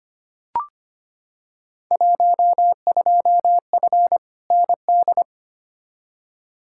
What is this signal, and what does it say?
Morse code, heard as a keyed tone, about 25 wpm, 700 Hz. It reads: 12F ND
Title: Field Day Exchanges Encore - 25wpm